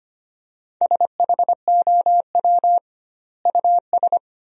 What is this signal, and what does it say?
Morse code, heard as a keyed tone, about 25 wpm, 700 Hz. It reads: SHOW US
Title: 2x Word-Spacing - Sentences from Top 200 Words - 25wpm